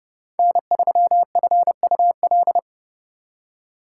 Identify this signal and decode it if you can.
Morse code, heard as a keyed tone, about 39 wpm, 700 Hz. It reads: N3FUL